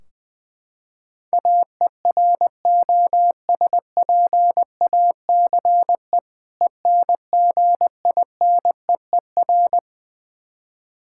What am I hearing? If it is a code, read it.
Morse code, heard as a keyed tone, about 21 wpm, 700 Hz. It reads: AEROSPACE ENGINEER
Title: K1USN SST QSO Practice 20wpm